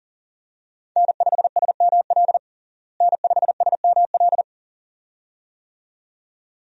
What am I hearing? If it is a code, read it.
Morse code, heard as a keyed tone, about 40 wpm, 700 Hz. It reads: N5SML N5SML
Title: USA Call Signs - 5 Characters - Repeated 2x - 40wpm